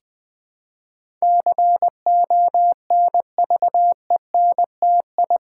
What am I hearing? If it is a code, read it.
Morse code, heard as a keyed tone, about 20 wpm, 700 Hz. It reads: CONVENTI
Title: Build A Word - 4 Letters At A Time - 10-Letter Words - 20wpm